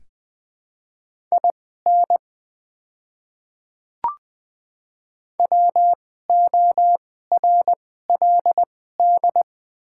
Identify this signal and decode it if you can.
Morse code, heard as a keyed tone, about 20 wpm, spaced as for 14 wpm, 690 Hz. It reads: IN WORLD
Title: 2x Character Spacing - Top 300 Words - 20wpm